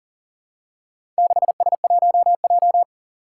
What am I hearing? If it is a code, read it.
Morse code, heard as a keyed tone, about 42 wpm, 700 Hz. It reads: 6S1J
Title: Four Challenging Letter-Number Combinations 40wpm